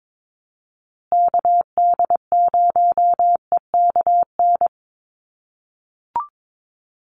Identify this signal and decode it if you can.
Morse code, heard as a keyed tone, about 22 wpm, 700 Hz. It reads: KD0EKN
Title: USA Call Signs Repeated 3x 22wpm